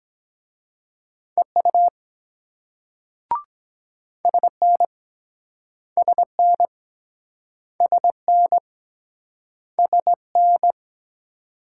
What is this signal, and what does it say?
Morse code, heard as a keyed tone, about 23 wpm, 700 Hz. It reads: EU SN SN SN SN